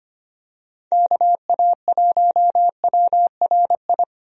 Morse code, heard as a keyed tone, about 25 wpm, 700 Hz. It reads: KA1WRI
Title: USA Call Signs - 6 Characters - Repeated 2x - 25wpm